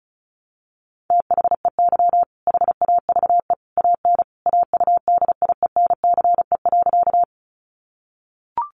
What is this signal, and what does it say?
Morse code, heard as a keyed tone, about 34 wpm, 700 Hz. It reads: THEY HAVE AN AUDIENCE.
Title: Sentences from Top 800 Words 35wpm